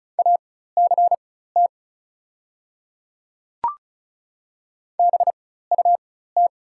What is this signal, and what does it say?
Morse code, heard as a keyed tone, about 36 wpm, spaced as for 16 wpm, 700 Hz. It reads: ACT BUT
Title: 4x Character Spacing - Top 200 Words - 35wpm